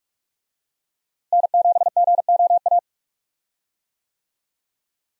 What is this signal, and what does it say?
Morse code, heard as a keyed tone, about 43 wpm, 700 Hz. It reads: N7GOA
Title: USA Call Signs - 5 Characters - Repeated 2x - 45wpm